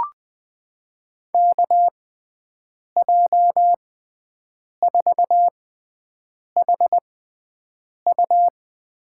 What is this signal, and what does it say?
Morse code, heard as a keyed tone, about 20 wpm, spaced as for 7 wpm, 700 Hz. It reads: KJ4HU
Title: USA Call Signs - 6 Characters - 6x Character Spacing - 20wpm